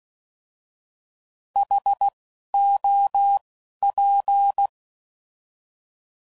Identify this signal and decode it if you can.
Morse code, heard as a keyed tone, about 16 wpm, 800 Hz. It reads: H O P